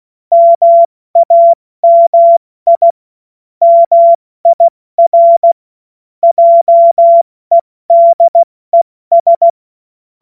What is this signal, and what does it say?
Morse code, heard as a keyed tone, about 16 wpm, 680 Hz. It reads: MAMI MIR JEDES
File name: Morsekurs / Witze / Witz 16. Mai 2023 70 F80